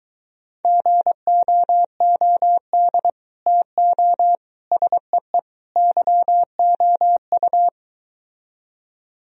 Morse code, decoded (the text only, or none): GOOD TO SEE YOU